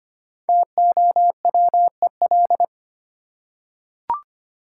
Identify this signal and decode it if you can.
Morse code, heard as a keyed tone, about 25 wpm, 690 Hz. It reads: TOWEL
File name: practice / 25wpm / Easy Words with Hints - 25wpm